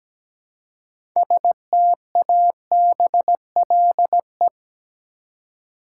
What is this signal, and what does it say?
Morse code, heard as a keyed tone, about 17 wpm, 700 Hz. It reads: STABLE
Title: Six Letter Words - 17wpm